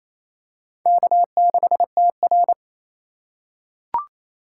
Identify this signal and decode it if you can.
Morse code, heard as a keyed tone, about 28 wpm, 690 Hz. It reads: K6TR
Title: USA Call Signs - 4 Characters - 2x Character Spacing - 28wpm